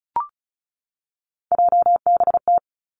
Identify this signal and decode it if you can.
Morse code, heard as a keyed tone, about 36 wpm, 700 Hz. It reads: JBT
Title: Three Challenging Letter-Number Combinations 35wpm